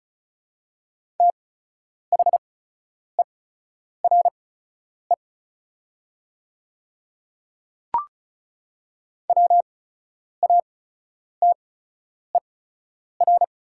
Morse code, heard as a keyed tone, about 34 wpm, spaced as for 10 wpm, 700 Hz. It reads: THERE WATER
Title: 8x Character Spacing - Top 300 Words - 35wpm